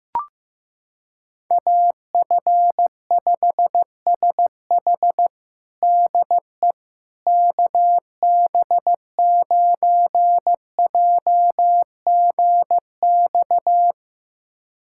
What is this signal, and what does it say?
Morse code, heard as a keyed tone, about 15 wpm, 700 Hz. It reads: AF5SH DE KB9JGX
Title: USA Callsign DE Callsign 15wpm